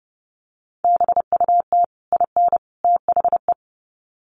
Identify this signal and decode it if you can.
Morse code, heard as a keyed tone, about 30 wpm, 700 Hz. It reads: BUT IN THE
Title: Sets of 3 Words 30wpm